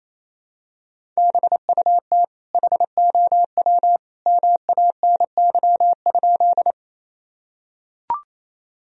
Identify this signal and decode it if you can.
Morse code, heard as a keyed tone, about 28 wpm, 700 Hz. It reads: BUT HOW MANY?
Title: Sentences from Top 1000 Words - Review Part 1 - 28wpm